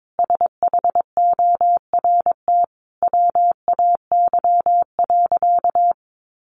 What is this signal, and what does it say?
Morse code, heard as a keyed tone, about 22 wpm, 700 Hz. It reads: SHORT WAY.
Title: Sentences from Top 1400 Words 22wpm